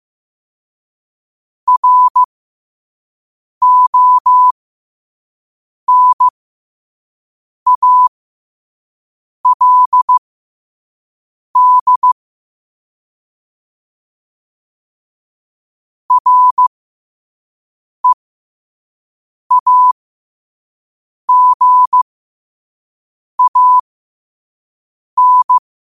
Morse code, heard as a keyed tone, about 15 wpm, spaced as for 5 wpm, 1000 Hz. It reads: RONALD REAGAN
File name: 5wpm